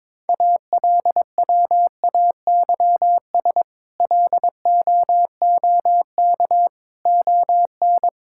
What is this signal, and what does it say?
Morse code, heard as a keyed tone, about 22 wpm, 700 Hz. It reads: ALWAYS LOOK ON
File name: practice / 22wpm / Sets of 8 Words - 22wpm